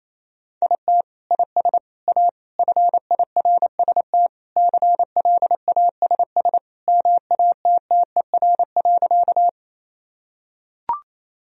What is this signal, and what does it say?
Morse code, heard as a keyed tone, about 28 wpm, 700 Hz. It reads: IT IS A FIRST CLASS MATTER.
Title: Sentences from Top 1000 Words - Review Part 2 - 28wpm